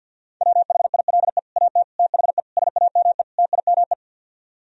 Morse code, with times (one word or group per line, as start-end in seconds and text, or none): 0.4-1.4 WHILE
1.6-1.8 AT
2.0-2.4 THE
2.6-3.2 SAME
3.4-3.9 TIME